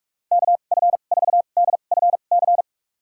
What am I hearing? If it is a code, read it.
Morse code, heard as a keyed tone, about 46 wpm, spaced as for 32 wpm, 700 Hz. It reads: KF4DFC
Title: USA Call Signs - 6 Characters - 2x Character Spacing - 45wpm